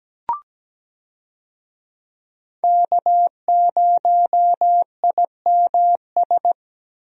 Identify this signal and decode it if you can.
Morse code, heard as a keyed tone, about 17 wpm, 700 Hz. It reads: K0IMS
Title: USA Call Signs - 5 Characters - Repeated 2x - 17wpm